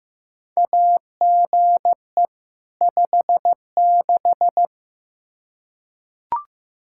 Morse code, heard as a keyed tone, about 15 wpm, 700 Hz. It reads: AGE 56
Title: K1USN SST QSO Practice 15wpm